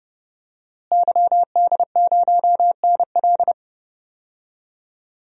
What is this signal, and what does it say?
Morse code, heard as a keyed tone, about 31 wpm, 700 Hz. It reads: YD0NL
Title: DXCC-100 Call Signs - 5 Characters - Repeated 2x - 30wpm